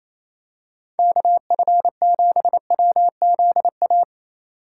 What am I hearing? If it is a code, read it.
Morse code, heard as a keyed tone, about 28 wpm, 700 Hz. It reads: KF7WZA
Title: USA Call Signs - 4x Character Spacing - 28wpm